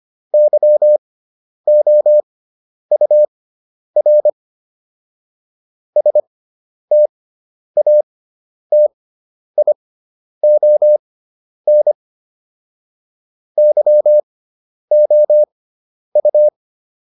Morse code, heard as a keyed tone, about 25 wpm, spaced as for 10 wpm, 600 Hz. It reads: YOUR STATION YOU